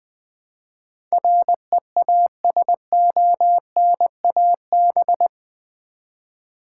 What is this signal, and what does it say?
Morse code, heard as a keyed tone, about 20 wpm, 700 Hz. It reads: REASONAB
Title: Build A Word - 4 Letters At A Time - 10-Letter Words - 20wpm